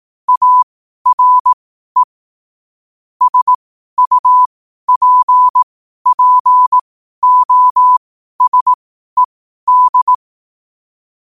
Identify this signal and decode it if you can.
Morse code, heard as a keyed tone, about 18 wpm, spaced as for 12 wpm, 1000 Hz. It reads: ARE SUPPOSED
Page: 12-wpm-morse-code-quotes